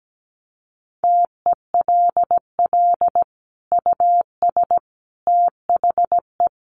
Morse code, heard as a keyed tone, about 17 wpm, 700 Hz. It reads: TELL US THE
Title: Sentences from Top 1400 Words 17wpm